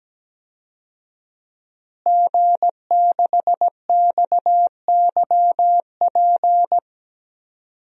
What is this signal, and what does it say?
Morse code, heard as a keyed tone, about 17 wpm, 700 Hz. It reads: G6XYP
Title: DXCC-100 Call Signs - 5 Characters - Repeated 3x - 17wpm